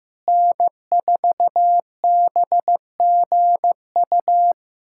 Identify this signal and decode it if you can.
Morse code, heard as a keyed tone, about 16 wpm, 700 Hz. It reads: N4BGU
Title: USA Callsign DE Callsign - 5 Characters - 15wpm